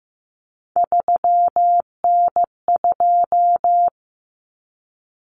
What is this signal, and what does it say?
Morse code, heard as a keyed tone, about 15 wpm, 700 Hz. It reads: 3N2
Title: Common Cut Numbers - 3-Digit 15wpm